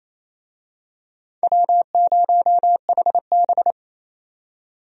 Morse code, heard as a keyed tone, about 28 wpm, 700 Hz. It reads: W0HB
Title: USA Call Signs - 4 Characters - 28wpm